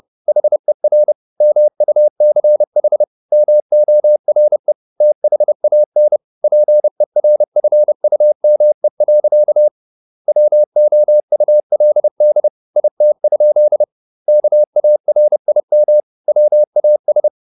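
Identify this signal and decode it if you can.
Morse code, heard as a keyed tone, about 30 wpm, 600 Hz. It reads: HER MUCH MORE THAN PERFUME. WOULD IT? KARIM WAS
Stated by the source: Morse code at 30 wpm, the novel Never by Ken Follett in English, from CW med Gnister.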